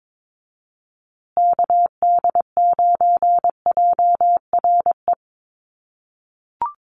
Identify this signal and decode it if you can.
Morse code, heard as a keyed tone, about 22 wpm, 700 Hz. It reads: KD9JRE
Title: USA Call Signs Repeated 2x 22wpm